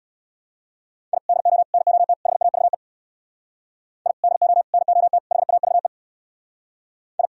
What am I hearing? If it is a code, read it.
Morse code, heard as a keyed tone, about 64 wpm, 700 Hz. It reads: I DO NOT HAVE I DO NOT HAVE I